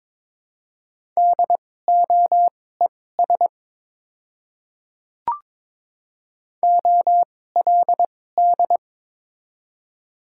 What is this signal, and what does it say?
Morse code, heard as a keyed tone, about 22 wpm, spaced as for 16 wpm, 700 Hz. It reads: DOES OLD
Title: 2x Character Spacing - Top 200 Words - 22wpm